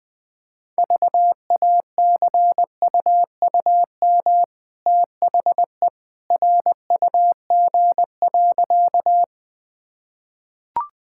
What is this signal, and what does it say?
Morse code, heard as a keyed tone, about 20 wpm, 700 Hz. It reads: VACUUM THE RUG.
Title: Easy Sentences with Hints - 20wpm